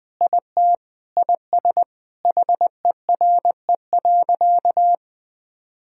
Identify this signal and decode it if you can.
Morse code, heard as a keyed tone, about 20 wpm, 700 Hz. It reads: IT IS HERE.
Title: Sentences from Top 1000 Words - Review Part 2 - 20wpm